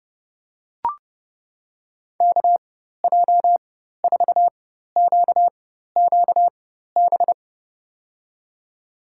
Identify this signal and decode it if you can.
Morse code, heard as a keyed tone, about 30 wpm, spaced as for 14 wpm, 700 Hz. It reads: KJ4QQB
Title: USA Call Signs - 4x Character Spacing - 30wpm